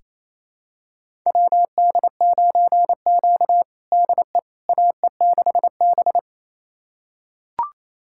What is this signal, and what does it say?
Morse code, heard as a keyed tone, about 28 wpm, 700 Hz. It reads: WD9Q DE AE6B
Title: USA Callsign DE Callsign - 4 Characters - 28wpm